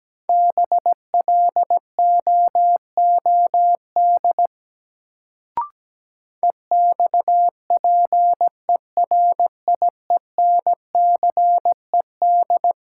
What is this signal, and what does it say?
Morse code, heard as a keyed tone, about 17 wpm, 700 Hz. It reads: BLOOD EXPERIENCED
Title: Sentences from Top 1100 Words - New Word Drill Alt - 17wpm